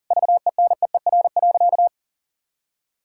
Morse code, heard as a keyed tone, about 40 wpm, 700 Hz. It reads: VENEER.
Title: Pangram Challenge v2 - 40wpm